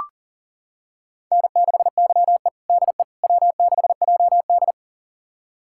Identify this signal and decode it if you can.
Morse code, heard as a keyed tone, about 40 wpm, 700 Hz. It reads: N6YE DE W6JD
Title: USA Callsign DE Callsign - 4 Characters - 40wpm